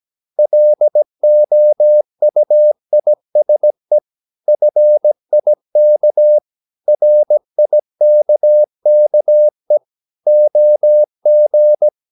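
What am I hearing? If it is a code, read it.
Morse code, heard as a keyed tone, about 17 wpm, 600 Hz. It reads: LOUISE FIK RIKKE OG
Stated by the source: Slow (QRS) Morse practice at 17 wpm, from CW med Gnister.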